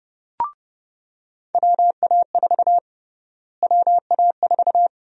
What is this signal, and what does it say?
Morse code, heard as a keyed tone, about 30 wpm, 700 Hz. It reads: WA4 WA4
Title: USA Call Sign Prefixes - Repeated 2x - 30wpm